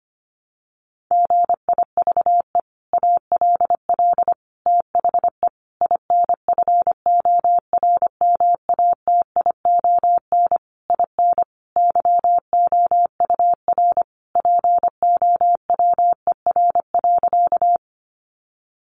Morse code, heard as a keyed tone, about 25 wpm, 700 Hz. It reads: GIVE ALL THE INFORMATION IN YOUR POWER.
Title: Sentences from Top 600 Words 25wpm